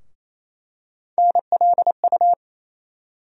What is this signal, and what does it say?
Morse code, heard as a keyed tone, about 28 wpm, 700 Hz. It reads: NLU
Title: USA Call Sign Postfixes - 3 Letters 28wpm